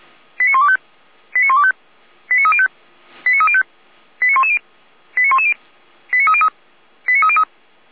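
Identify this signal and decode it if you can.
Five-tone selective call (ZVEI), heard as quick runs of five tones, tones of about 70 ms. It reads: 89125 89125 89305 89305 89110 89110 89303 89303
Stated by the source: Five-tone ZVEI selective call, received with an SDR.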